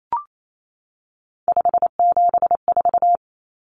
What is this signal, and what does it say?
Morse code, heard as a keyed tone, about 28 wpm, 700 Hz. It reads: 574
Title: Common Cut Numbers - 3-Digit 28wpm